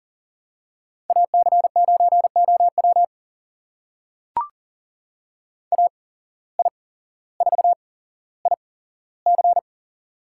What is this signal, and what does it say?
Morse code, heard as a keyed tone, about 40 wpm, spaced as for 17 wpm, 700 Hz. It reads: AC9OW AI4IC
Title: USA Call Signs - 5 Characters - 8x Character Spacing - 40wpm